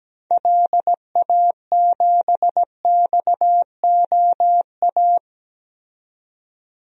Morse code, heard as a keyed tone, about 17 wpm, 700 Hz. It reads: LA7XOA
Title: DXCC-100 Call Signs - 6 Characters - 17wpm